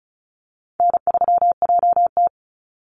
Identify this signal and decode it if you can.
Morse code, heard as a keyed tone, about 34 wpm, 700 Hz. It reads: N3JT
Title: Easy Call Signs Repeated 2x - 35wpm